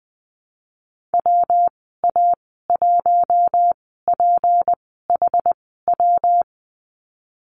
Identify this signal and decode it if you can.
Morse code, heard as a keyed tone, about 20 wpm, spaced as for 15 wpm, 700 Hz. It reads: WA1PHW
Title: 161 Lesson 26g - Letter g - Call Signs 20wpm